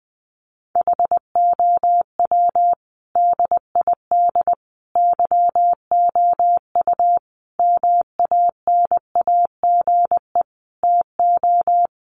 Morse code, heard as a keyed tone, about 20 wpm, 700 Hz. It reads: HOW DID YOU MANAGE TO
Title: Sentences from Top 1400 Words 20wpm